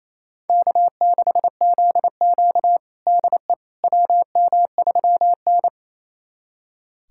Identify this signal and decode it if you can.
Morse code, heard as a keyed tone, about 27 wpm, 700 Hz. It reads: K6ZQ DE WM3N
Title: USA Callsign DE Callsign - 4 Characters - 28wpm